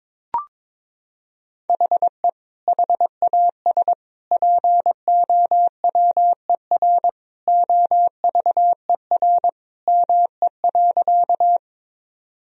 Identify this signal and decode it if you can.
Morse code, heard as a keyed tone, about 22 wpm, 700 Hz. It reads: HE HAS POWER OVER ME.
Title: Sentences from Top 1000 Words - Review Part 2 - 22wpm